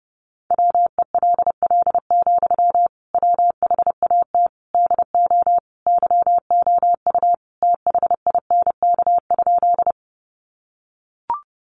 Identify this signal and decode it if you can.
Morse code, heard as a keyed tone, about 30 wpm, 700 Hz. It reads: WELL, WHAT DO YOU THINK?
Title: Sentences from Top 200 Words 30wpm